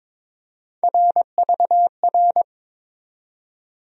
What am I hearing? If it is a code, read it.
Morse code, heard as a keyed tone, about 22 wpm, 700 Hz. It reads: RVR
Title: USA Call Sign Postfixes - 3 Letters 22wpm